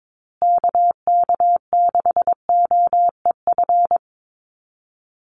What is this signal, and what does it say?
Morse code, heard as a keyed tone, about 22 wpm, 700 Hz. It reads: KK6OEF
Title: USA Call Signs Repeated 3x 22wpm